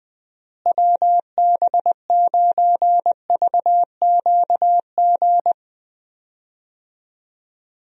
Morse code, heard as a keyed tone, about 19 wpm, 700 Hz. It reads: WB9VQG
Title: USA Call Signs - 6 Characters - Repeated 2x - 20wpm